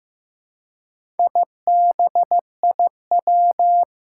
Morse code, heard as a keyed tone, about 15 wpm, 700 Hz. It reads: IBIW
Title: Four Challenging Letter-Number Combinations 15wpm